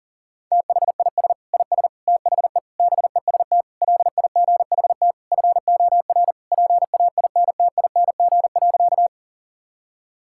Morse code, heard as a keyed tone, about 40 wpm, 700 Hz. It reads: THIS IS THE BEST LIGHT FOR PAINTING.
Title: Sentences from Top 1000 Words - Review Part 6 - 40wpm